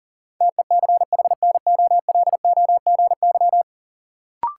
Morse code, heard as a keyed tone, about 40 wpm, 700 Hz. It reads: TECHNOLOGY
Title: Sentences from Top 1200 Words - New Word Drill Alt - 40wpm